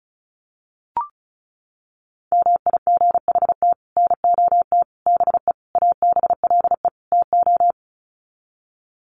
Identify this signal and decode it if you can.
Morse code, heard as a keyed tone, about 34 wpm, 700 Hz. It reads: MIGHT NOT BE ABLE TO
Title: Sets of 5 Words Encore2 35wpm